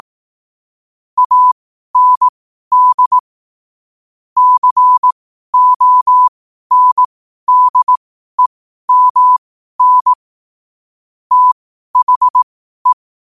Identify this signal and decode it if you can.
Morse code, heard as a keyed tone, about 18 wpm, spaced as for 12 wpm, 1000 Hz. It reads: AND CONDEMN THE
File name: qotdfiles / 12wpm